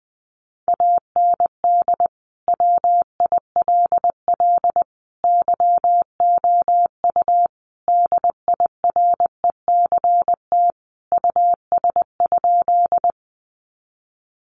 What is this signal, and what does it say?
Morse code, heard as a keyed tone, about 20 wpm, 700 Hz. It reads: AND WILL YOU DIRECT US?
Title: Sentences from Top 400 Words 20wpm